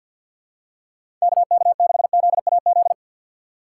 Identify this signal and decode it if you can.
Morse code, heard as a keyed tone, about 52 wpm, 700 Hz. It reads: KK6ZAZ
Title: USA Call Signs - 4x Character Spacing - 50wpm